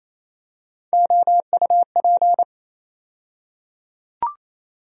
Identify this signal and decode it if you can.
Morse code, heard as a keyed tone, about 28 wpm, 700 Hz. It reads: OUP
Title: USA Call Sign Postfixes - 3 Letters 28wpm